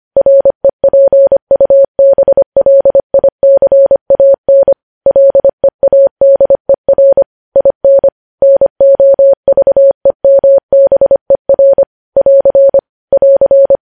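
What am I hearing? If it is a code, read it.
Morse code, heard as a keyed tone, about 25 wpm, 550 Hz. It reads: REPUBLICAN LEADER IN NOVEMBER <AR> <AR>
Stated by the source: News headlines sent in Morse at 25 wpm.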